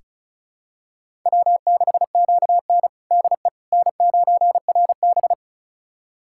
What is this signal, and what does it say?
Morse code, heard as a keyed tone, about 36 wpm, 700 Hz. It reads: W6QN DE N9RB
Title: USA Callsign DE Callsign - 4 Characters - 35wpm